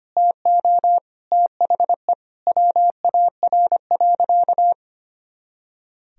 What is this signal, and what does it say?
Morse code, heard as a keyed tone, about 25 wpm, 700 Hz. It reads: TO THE WAR.
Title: Sentences from Top 400 Words and Abbreviations 25wpm